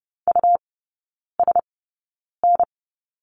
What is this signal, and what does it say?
Morse code, heard as a keyed tone, about 30 wpm, spaced as for 9 wpm, 700 Hz. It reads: USN
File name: Lesson 17e - Letter u - Three Characters 30wpm